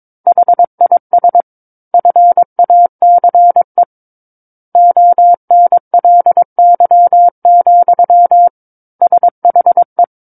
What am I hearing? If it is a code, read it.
Morse code, heard as a keyed tone, about 22 wpm, 700 Hz. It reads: HIS FACE ONLY, SHE
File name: 06-Chapter-06 22 WPM